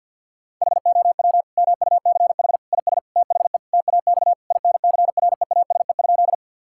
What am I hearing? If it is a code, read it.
Morse code, heard as a keyed tone, about 50 wpm, 700 Hz. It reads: HOW MUCH IS THE TAX INCREASE?
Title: Sentences from Top 1000 Words - Review Part 5 - 50wpm